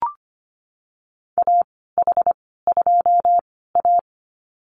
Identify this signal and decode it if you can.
Morse code, heard as a keyed tone, about 25 wpm, spaced as for 16 wpm, 700 Hz. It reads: AH2A
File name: Lesson 15g - Number 5 - Call Signs 25wpm